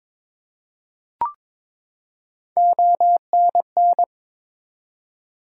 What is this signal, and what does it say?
Morse code, heard as a keyed tone, about 22 wpm, 700 Hz. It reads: ONN
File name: ARRL Sections v2 - 22wpm